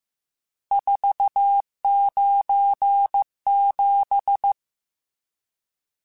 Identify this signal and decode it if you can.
Morse code, heard as a keyed tone, about 15 wpm, 800 Hz. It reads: 497
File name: RSTs / RST-mp3s / Various RST @13